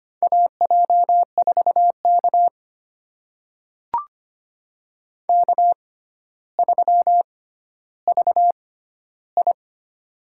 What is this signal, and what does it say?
Morse code, heard as a keyed tone, about 25 wpm, spaced as for 13 wpm, 700 Hz. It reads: AJ4K K3VI
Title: USA Call Signs - 4 Characters - 6x Character Spacing - 25wpm